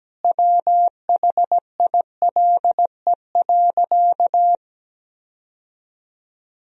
Morse code, heard as a keyed tone, about 17 wpm, 700 Hz. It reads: WHILE.
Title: Sentences from Top 1000 Words - Review Part 1 - 17wpm